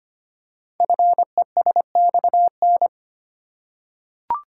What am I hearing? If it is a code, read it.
Morse code, heard as a keyed tone, about 25 wpm, 700 Hz. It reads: FESXN
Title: Code Groups - 5 Letters - 25wpm